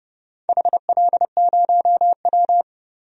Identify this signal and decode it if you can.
Morse code, heard as a keyed tone, about 30 wpm, 700 Hz. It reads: HL0W
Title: DXCC-100 Call Signs - 4 Characters - 30wpm